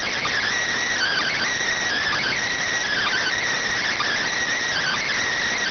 Emergency Alert System (SAME) tones (header bursts), heard as harsh buzzing bursts. It is heard from 0.9 to 5.2 s.